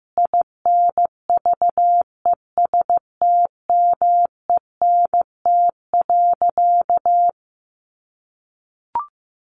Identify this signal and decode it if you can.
Morse code, heard as a keyed tone, about 15 wpm, 700 Hz. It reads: INVESTMENT.
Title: Sentences from Top 700 Words 15wpm